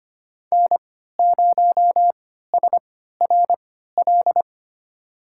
Morse code, heard as a keyed tone, about 25 wpm, spaced as for 14 wpm, 700 Hz. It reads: N0SRL
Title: USA Call Signs - 5 Characters - 3x Character Spacing - 25wpm